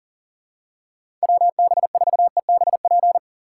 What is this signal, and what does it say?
Morse code, heard as a keyed tone, about 40 wpm, 700 Hz. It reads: WB4EBP
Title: USA Call Signs - 6 Characters - 6x Character Spacing - 40wpm